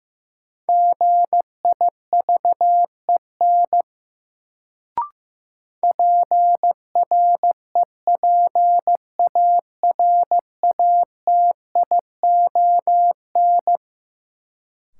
Morse code, heard as a keyed tone, about 15 wpm, 700 Hz. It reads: GIVEN PREPARATION
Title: Sentences from Top 1400 Words - New Word Drill - 15wpm